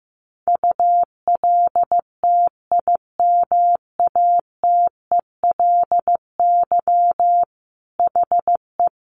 Morse code, heard as a keyed tone, about 15 wpm, 700 Hz. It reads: ULTIMATELY HE
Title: Sentences from Top 1200 Words 15wpm